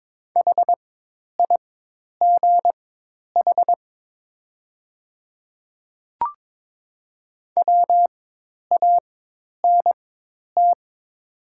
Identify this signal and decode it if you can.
Morse code, heard as a keyed tone, about 22 wpm, spaced as for 10 wpm, 700 Hz. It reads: HIGH WANT
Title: 4x Character Spacing - Top 200 Words - 22wpm